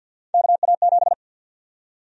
Morse code, heard as a keyed tone, about 51 wpm, 700 Hz. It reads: KA7
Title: USA Call Sign Prefixes - 50wpm